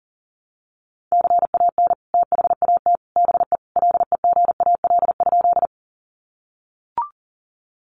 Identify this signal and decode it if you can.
Morse code, heard as a keyed tone, about 40 wpm, 700 Hz. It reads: CAN THAT BE LEGAL?
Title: Sentences from Top 1200 Words 40wpm